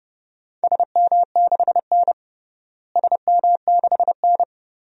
Morse code, heard as a keyed tone, about 30 wpm, 700 Hz. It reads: SM6N SM6N
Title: DXCC-100 Call Signs - 4 Characters - Repeated 2x - 30wpm